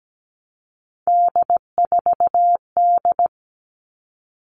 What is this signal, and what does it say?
Morse code, heard as a keyed tone, about 17 wpm, 700 Hz. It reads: D4D
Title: All Cut Numbers - 3-Digit 17wpm